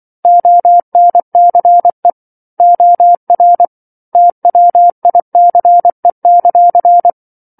Morse code, heard as a keyed tone, about 24 wpm, 700 Hz. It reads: ONCE OR TWICE;
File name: 07-Chapter-07 24 WPM